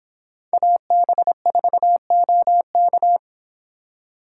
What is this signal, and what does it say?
Morse code, heard as a keyed tone, about 25 wpm, 700 Hz. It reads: AB4OK